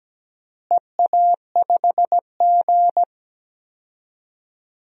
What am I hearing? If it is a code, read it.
Morse code, heard as a keyed tone, about 17 wpm, 700 Hz. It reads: EA5G
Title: DXCC-100 Call Signs - 4 Characters - Repeated 3x - 17wpm